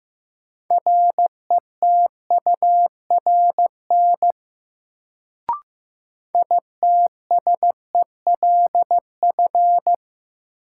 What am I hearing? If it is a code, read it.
Morse code, heard as a keyed tone, about 15 wpm, 700 Hz. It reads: RETURN ITSELF
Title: Sentences from Top 800 Words - New Word Drill - 15wpm